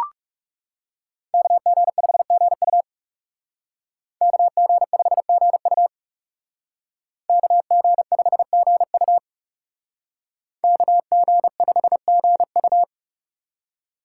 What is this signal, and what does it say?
Morse code, heard as a keyed tone, about 39 wpm, 700 Hz. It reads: KG5GU KG5GU KG5GU KG5GU